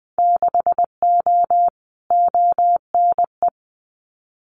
Morse code, heard as a keyed tone, about 20 wpm, 700 Hz. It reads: 6O ONE
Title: Winter Field Day Exchanges Encore 20wpm